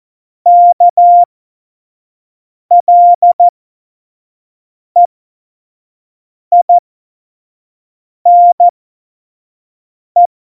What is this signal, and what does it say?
Morse code, heard as a keyed tone, about 14 wpm, spaced as for 5 wpm, 700 Hz. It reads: KLEINE
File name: Morsekurs / Witze / Witz 20 March 2018, 25 F70